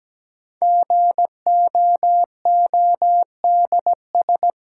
Morse code, heard as a keyed tone, about 17 wpm, 700 Hz. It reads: GOODS
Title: Sentences from Top 1000 Words - Review Part 6 - New Word Drill - 17wpm